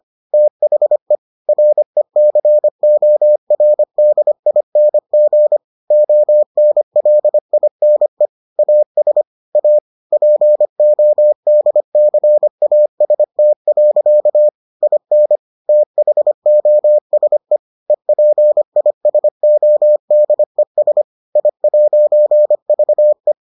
Morse code, heard as a keyed tone, about 25 wpm, 600 Hz. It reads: THE RECORDING ONLINE AS A PODCAST. IN THOSE EPISODES I'VE